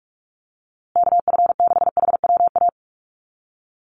Morse code, heard as a keyed tone, about 44 wpm, 700 Hz. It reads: KF6HWA
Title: USA Call Signs Repeated 2x 45wpm